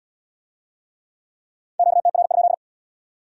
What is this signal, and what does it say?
Morse code, heard as a keyed tone, about 64 wpm, 700 Hz. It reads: QTR?